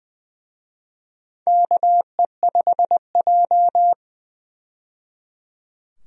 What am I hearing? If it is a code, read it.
Morse code, heard as a keyed tone, about 20 wpm, 700 Hz. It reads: KE5J
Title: USA Call Signs - 4 Characters - 20wpm